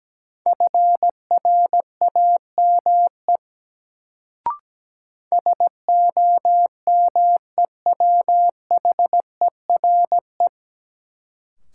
Morse code, heard as a keyed tone, about 17 wpm, 700 Hz. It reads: FRAME SOMEWHERE
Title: Sentences from Top 1400 Words - New Word Drill Alt - 17wpm